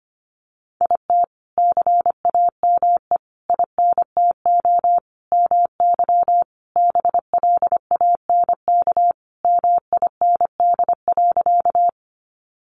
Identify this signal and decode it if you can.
Morse code, heard as a keyed tone, about 25 wpm, 700 Hz. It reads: IT CAME INTO MY BLANK MIND.
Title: Sentences from Top 1500 Words 25wpm